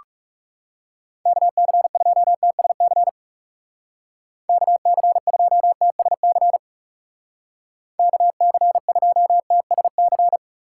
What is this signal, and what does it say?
Morse code, heard as a keyed tone, about 40 wpm, 700 Hz. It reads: KC2TSC KC2TSC KC2TSC